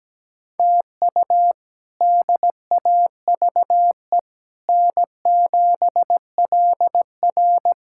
Morse code, heard as a keyed tone, about 17 wpm, 700 Hz. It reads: TU DAVE N7LR
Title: K1USN SST QSO Practice - 17wpm